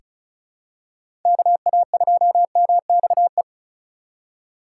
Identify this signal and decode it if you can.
Morse code, heard as a keyed tone, about 36 wpm, 700 Hz. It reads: KA2MXE
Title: USA Call Signs - 6 Characters - 3x Character Spacing - 35wpm